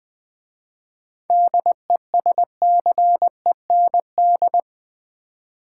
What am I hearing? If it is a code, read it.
Morse code, heard as a keyed tone, about 20 wpm, 690 Hz. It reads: DESCEND
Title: 7 Letter Words - First 3 Letters Given - 20wpm